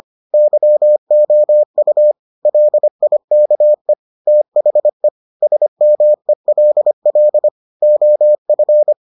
Morse code, heard as a keyed tone, about 25 wpm, 600 Hz. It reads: YOU LIKE THE SMELL OF